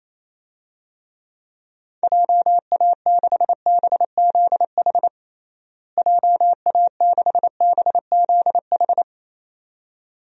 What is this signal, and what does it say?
Morse code, heard as a keyed tone, about 28 wpm, 700 Hz. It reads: JA6BZH JA6BZH
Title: DXCC-100 Call Signs - 6 Characters - Repeated 2x - 28wpm